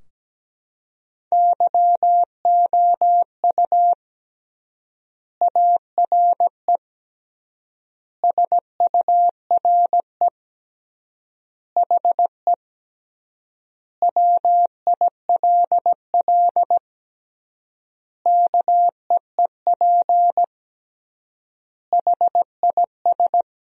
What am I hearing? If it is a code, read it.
Morse code, heard as a keyed tone, about 17 wpm, 700 Hz. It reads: YOU ARE SURE HE WILL KEEP HIS
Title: 3x Word-Spacing - Sentences from Top 300 Words - 17wpm